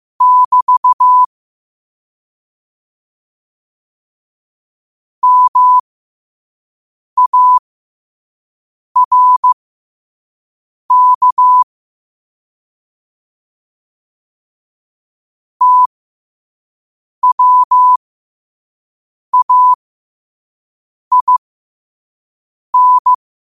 Morse code, heard as a keyed tone, about 15 wpm, spaced as for 5 wpm, 1000 Hz. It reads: <BT> MARK TWAIN
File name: qotdfiles / 5wpm